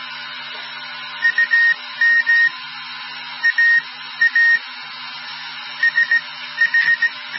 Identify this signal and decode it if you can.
Morse code, heard as a keyed tone, about 17 wpm, spaced as for 13 wpm, 1850 Hz. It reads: UM AA SH